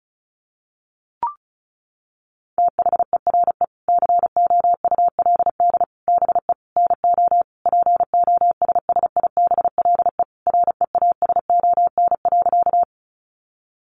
Morse code, heard as a keyed tone, about 34 wpm, 700 Hz. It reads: THERE COULD BE NO POSSIBLE REASON.
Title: Sentences from Top 700 Words 35wpm